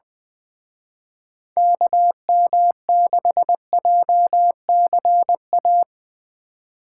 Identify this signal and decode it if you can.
Morse code, heard as a keyed tone, about 22 wpm, 700 Hz. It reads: KM6JCA
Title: Speed Racing - USA Call Signs 20wpm